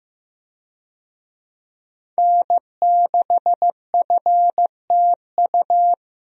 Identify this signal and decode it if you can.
Morse code, heard as a keyed tone, about 15 wpm, 700 Hz. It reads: N6FTU
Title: USA Call Signs - 5 Characters - Repeated 3x - 15wpm